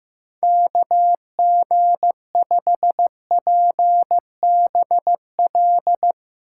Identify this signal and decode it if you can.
Morse code, heard as a keyed tone, about 15 wpm, 700 Hz. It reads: KG5PBL
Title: Speed Racing - USA 2x3 Call Signs 15wpm